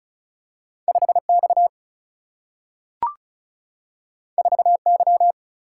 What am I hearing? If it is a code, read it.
Morse code, heard as a keyed tone, about 36 wpm, 700 Hz. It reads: 5X 4Y
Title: Single-Letter Single-Number 35wpm